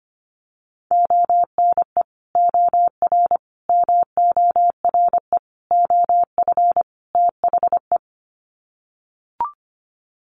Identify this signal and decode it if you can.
Morse code, heard as a keyed tone, about 25 wpm, 700 Hz. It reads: ONE OR MORE OF THE
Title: Sets of 5 Words Encore 25wpm